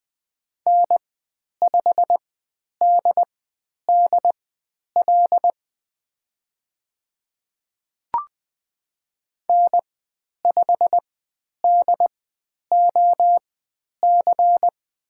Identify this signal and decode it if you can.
Morse code, heard as a keyed tone, about 20 wpm, spaced as for 10 wpm, 700 Hz. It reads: N5DDL N5DOC
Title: 102 Lesson 17g - Letter u - Call Signs 20wpm